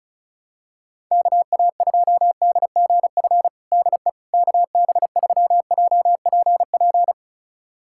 Morse code, heard as a keyed tone, about 36 wpm, 700 Hz. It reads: KA2DGF DE KB3JPP
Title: USA Callsign DE Callsign - 6 Characters - 35wpm